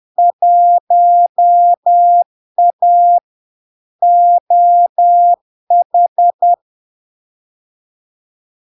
Morse code, heard as a keyed tone, about 10 wpm, 700 Hz. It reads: 1A OH
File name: LARK-FD-2022-Log-in-CW-10wpm-700-Hz